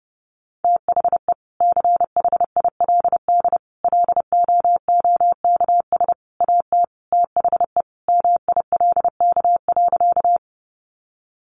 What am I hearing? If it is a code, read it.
Morse code, heard as a keyed tone, about 30 wpm, 700 Hz. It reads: THE CHILD LOOKS AT THE MILK.
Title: Additional Easy Phrases 30wpm